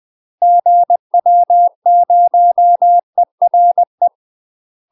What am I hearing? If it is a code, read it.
Morse code, heard as a keyed tone, about 20 wpm, 700 Hz. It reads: GW0ERE
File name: UK MASTER QSOs TRAINING / 20WPM / UK MASTER QSOs 0045